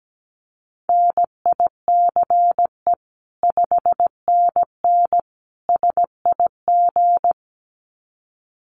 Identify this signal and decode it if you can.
Morse code, heard as a keyed tone, about 17 wpm, 700 Hz. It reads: NICE 5NN SIG
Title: Common QSO Phrases 17wpm